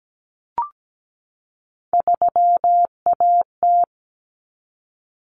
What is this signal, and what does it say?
Morse code, heard as a keyed tone, about 17 wpm, 700 Hz. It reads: 3AT
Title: Common Cut Numbers - 3-Digit 17wpm